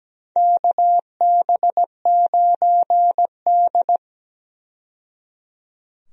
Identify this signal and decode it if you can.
Morse code, heard as a keyed tone, about 17 wpm, 700 Hz. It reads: KB9D
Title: USA Call Signs - 4 Characters - 17wpm